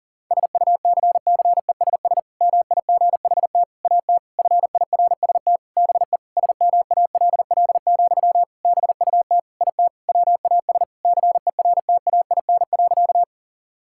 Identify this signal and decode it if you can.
Morse code, heard as a keyed tone, about 40 wpm, 700 Hz. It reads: SUCCESS MIGHT AT FIRST BE SMALL, BUT IT WAS CERTAIN.
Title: Sentences from Top 1000 Words - Review Part 6 - 40wpm